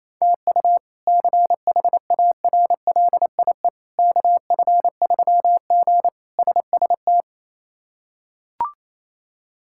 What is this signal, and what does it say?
Morse code, heard as a keyed tone, about 27 wpm, 700 Hz. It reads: TU CHARLIE KF3G SST
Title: K1USN SST QSO Practice - 28wpm